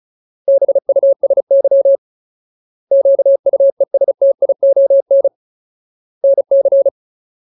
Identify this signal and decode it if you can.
Morse code, heard as a keyed tone, about 34 wpm, 550 Hz. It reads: BUSY QUESTION NC
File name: Ultimate Word List / 35WPM Ultimate